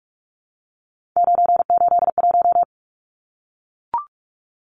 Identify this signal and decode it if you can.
Morse code, heard as a keyed tone, about 44 wpm, 700 Hz. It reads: 981
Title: Numbers - 3-Digit 45wpm